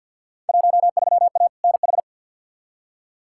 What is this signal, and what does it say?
Morse code, heard as a keyed tone, about 52 wpm, 700 Hz. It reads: 13A NH
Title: Field Day Exchanges Encore - 50wpm